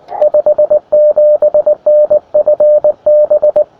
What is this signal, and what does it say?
Morse code, heard as a keyed tone, about 19 wpm, 600 Hz, with radio noise. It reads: 57NFB
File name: attachments / ua6ypt 15m